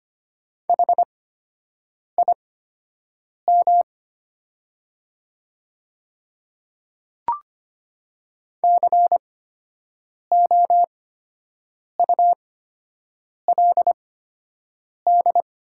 Morse code, heard as a keyed tone, about 25 wpm, spaced as for 7 wpm, 700 Hz. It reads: HIM COULD
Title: 8x Character Spacing - Top 100 Words - 25wpm